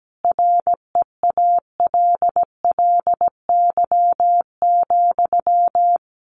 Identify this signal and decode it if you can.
Morse code, heard as a keyed tone, about 17 wpm, 700 Hz. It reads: REALLY,
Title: Sentences from Top 600 Words 17wpm